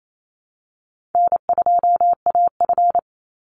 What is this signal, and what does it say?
Morse code, heard as a keyed tone, about 28 wpm, 700 Hz. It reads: N2AF
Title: NA SOTA Chaser Callsigns 28wpm